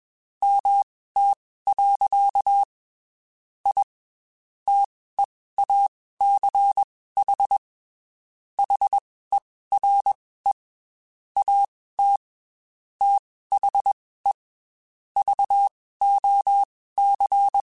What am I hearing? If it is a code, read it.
Morse code, heard as a keyed tone, about 21 wpm, spaced as for 15 wpm, 800 Hz. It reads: MT. I TEACH HERE AT THE VOC
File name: QSOs / QSOs-mp3s / WD0BGP 1@20